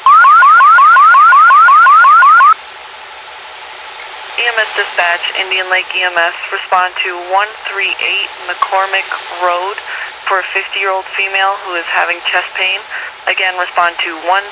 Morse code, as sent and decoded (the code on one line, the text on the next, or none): none